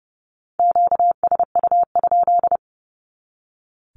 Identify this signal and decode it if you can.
Morse code, heard as a keyed tone, about 30 wpm, 700 Hz. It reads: QSU?
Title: Q-Codes 30wpm